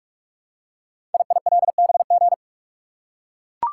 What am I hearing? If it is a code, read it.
Morse code, heard as a keyed tone, about 44 wpm, 700 Hz. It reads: IILBG
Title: Code Groups - 5 Letters - 45wpm